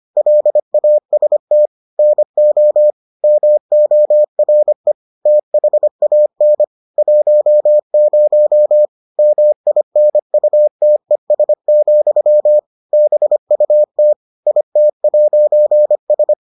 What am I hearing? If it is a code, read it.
Morse code, heard as a keyed tone, about 25 wpm, 600 Hz. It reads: LAST NO MORE THAN 10 MINUTES, BUT IT'S